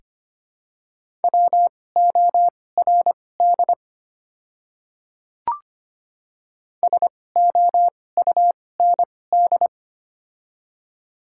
Morse code, heard as a keyed tone, about 25 wpm, spaced as for 18 wpm, 700 Hz. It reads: WORD SOUND
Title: 2x Character Spacing - Top 300 Words - 25wpm